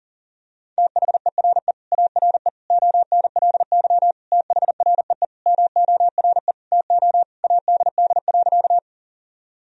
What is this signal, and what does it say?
Morse code, heard as a keyed tone, about 40 wpm, 700 Hz. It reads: THERE ARE ONLY THREE MORE TO ADD.
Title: Sentences from Top 1000 Words - Review Part 1 - 40wpm